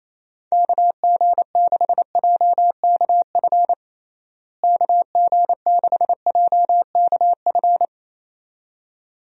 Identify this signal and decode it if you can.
Morse code, heard as a keyed tone, about 27 wpm, 700 Hz. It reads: KG6JKF KG6JKF
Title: USA Call Signs - 6 Characters - Repeated 2x - 28wpm